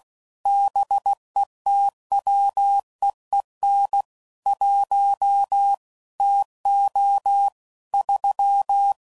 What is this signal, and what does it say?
Morse code, heard as a keyed tone, about 16 wpm, 800 Hz. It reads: BETWEEN 1 TO 3
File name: CW Code Groups @15WPM / Weather Sets / Weather 5 5